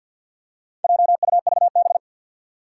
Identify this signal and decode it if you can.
Morse code, heard as a keyed tone, about 51 wpm, 700 Hz. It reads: JUVB
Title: Four Challenging Letter-Number Combinations 50wpm